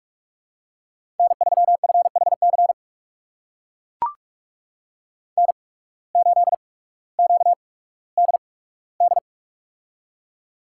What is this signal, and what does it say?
Morse code, heard as a keyed tone, about 42 wpm, 690 Hz. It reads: N 3 F H C N 8 Q D D
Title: USA Call Signs - 8x Character Spacing - 45wpm